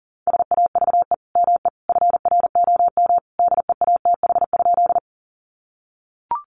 Morse code, heard as a keyed tone, about 37 wpm, 700 Hz. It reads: SAVE ME FROM DEATH?
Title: Sentences from Top 1100 Words 40wpm